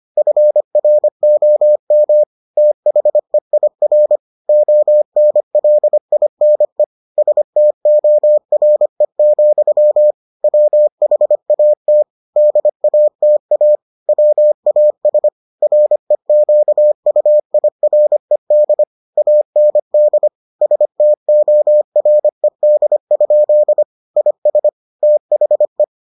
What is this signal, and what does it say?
Morse code, heard as a keyed tone, about 25 wpm, 600 Hz. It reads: FROM THEIR ONLINE STORE, WHAT DATA WAS REQUIRED AND STORED? IS THE